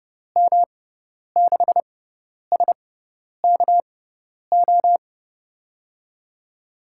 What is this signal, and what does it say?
Morse code, heard as a keyed tone, about 30 wpm, spaced as for 10 wpm, 700 Hz. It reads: M6SKO
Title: USA Call Signs - 6 Characters - 6x Character Spacing - 30wpm